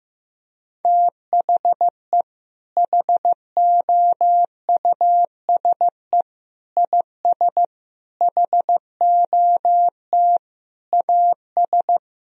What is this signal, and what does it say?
Morse code, heard as a keyed tone, about 15 wpm, 700 Hz. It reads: THE HOUSE IS HOT AS
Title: Sentences from Top 1000 Words - Review Part 4 - 15wpm